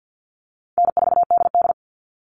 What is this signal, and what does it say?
Morse code, heard as a keyed tone, about 50 wpm, 700 Hz. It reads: N4DD
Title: Easy Call Signs - 50wpm